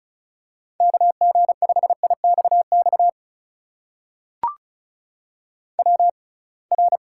Morse code, heard as a keyed tone, about 36 wpm, spaced as for 27 wpm, 700 Hz. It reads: KG5IXX WR
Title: USA Call Signs - 6x Character Spacing - 35wpm